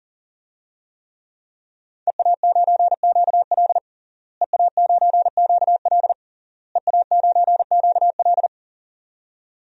Morse code, heard as a keyed tone, about 40 wpm, 700 Hz. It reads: EA9QL EA9QL EA9QL
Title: DXCC-100 Call Signs - 5 Characters - Repeated 3x - 40wpm